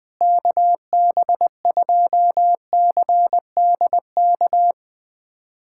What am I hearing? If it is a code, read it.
Morse code, heard as a keyed tone, about 20 wpm, 690 Hz. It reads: KB2CDK
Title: USA Callsign DE Callsign 20wpm